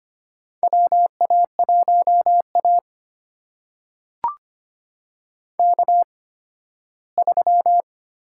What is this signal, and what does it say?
Morse code, heard as a keyed tone, about 25 wpm, spaced as for 15 wpm, 700 Hz. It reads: WA1A K3
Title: USA Call Signs - 4 Characters - 8x Character Spacing - 25wpm